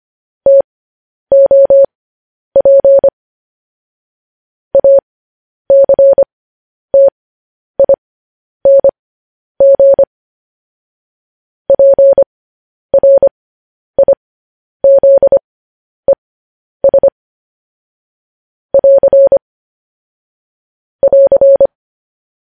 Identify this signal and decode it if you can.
Morse code, heard as a keyed tone, about 25 wpm, spaced as for 10 wpm, 550 Hz. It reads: TOP ACTING PRIZES <AR> <AR>
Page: news headlines in morse code at 10 wpm